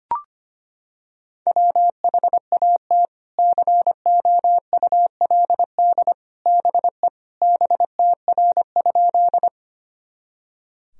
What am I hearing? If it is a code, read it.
Morse code, heard as a keyed tone, about 25 wpm, 700 Hz. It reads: WHAT COULD BE BTR?
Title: Sentences from Top 400 Words and Abbreviations 25wpm